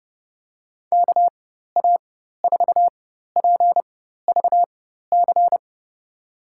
Morse code, heard as a keyed tone, about 30 wpm, spaced as for 14 wpm, 700 Hz. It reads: KA4PVC
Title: USA Call Signs - 6 Characters - 4x Character Spacing - 30wpm